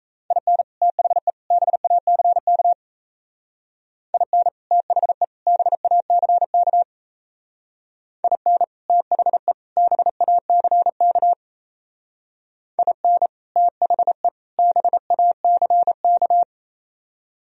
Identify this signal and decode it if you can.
Morse code, heard as a keyed tone, about 34 wpm, 700 Hz. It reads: IN THE BACK IN THE BACK IN THE BACK IN THE BACK
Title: Speed-Racing - Sets of 3 Words 28wpm